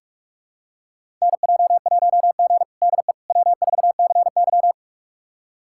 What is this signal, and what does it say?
Morse code, heard as a keyed tone, about 42 wpm, 700 Hz. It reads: NJ1G DE W4CY
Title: USA Callsign DE Callsign - 4 Characters - 45wpm